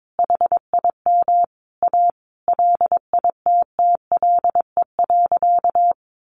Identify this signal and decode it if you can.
Morse code, heard as a keyed tone, about 22 wpm, 700 Hz. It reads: HIM A LITTLE.
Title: Sentences from Top 1500 Words 22wpm